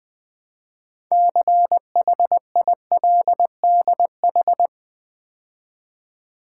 Morse code, heard as a keyed tone, about 20 wpm, 700 Hz. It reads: CHILDH
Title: Build A Word - 3 Letters At A Time - 9-Letter Words - 20wpm